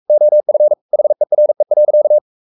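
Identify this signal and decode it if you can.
Morse code, heard as a keyed tone, about 44 wpm, 600 Hz. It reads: OF HERE.
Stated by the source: Morse code at 43 wpm sending Ken Follett's novel Never in English, from CW med Gnister.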